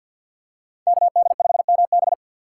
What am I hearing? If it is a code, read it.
Morse code, heard as a keyed tone, about 48 wpm, 700 Hz. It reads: KD5MB
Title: USA Call Signs - 5 Characters - 4x Character Spacing - 50wpm